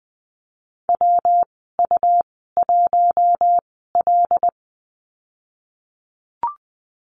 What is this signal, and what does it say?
Morse code, heard as a keyed tone, about 20 wpm, spaced as for 14 wpm, 700 Hz. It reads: WU1L
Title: 161 Lesson 26g - Letter g - Call Signs 20wpm